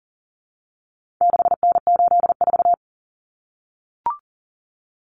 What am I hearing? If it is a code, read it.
Morse code, heard as a keyed tone, about 40 wpm, 700 Hz. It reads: 6N84
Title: Common Cut Numbers - 4-Digit 40wpm